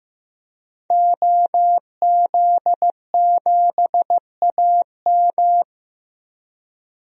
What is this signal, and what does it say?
Morse code, heard as a keyed tone, about 15 wpm, 700 Hz. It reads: OZ7AM
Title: DXCC-100 Call Signs - 5 Characters - 15wpm